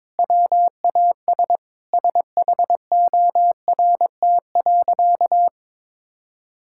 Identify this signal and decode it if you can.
Morse code, heard as a keyed tone, about 22 wpm, 700 Hz. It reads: WAS SHORT.
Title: Sentences from Top 1000 Words - Review Part 6 - 22wpm